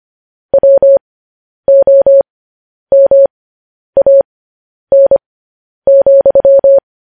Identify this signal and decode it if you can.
Morse code, heard as a keyed tone, about 25 wpm, spaced as for 10 wpm, 550 Hz. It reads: WOMAN,
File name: podcast / News headlines-1x-25x10